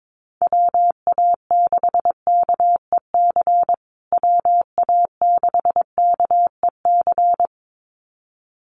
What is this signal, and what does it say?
Morse code, heard as a keyed tone, about 22 wpm, 700 Hz. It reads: WA6KEC WA6KEC
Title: USA Call Signs Repeated 2x 22wpm